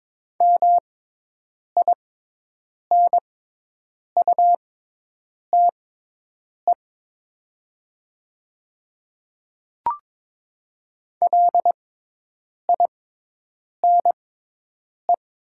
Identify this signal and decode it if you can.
Morse code, heard as a keyed tone, about 22 wpm, spaced as for 8 wpm, 700 Hz. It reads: MINUTE LINE
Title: 6x Character Spacing - Top 500 Words - 22wpm